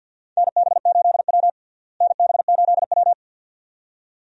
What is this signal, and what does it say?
Morse code, heard as a keyed tone, about 52 wpm, 700 Hz. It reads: NB8W NB8W
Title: USA Call Signs - 4 Characters - Repeated 2x - 50wpm